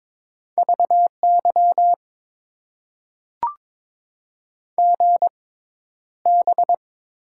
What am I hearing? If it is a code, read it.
Morse code, heard as a keyed tone, about 22 wpm, spaced as for 11 wpm, 700 Hz. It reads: VY GB
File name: Top 90 QSO Elements - 6x Character Spacing - 22wpm